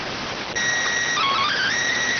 SSTV (slow-scan television), PD 290.